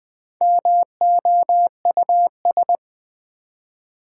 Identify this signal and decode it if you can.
Morse code, heard as a keyed tone, about 20 wpm, 700 Hz. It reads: MOUS
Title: Build A Word - 1 Letter At A Time - 3 to 5-Letter Words - 20wpm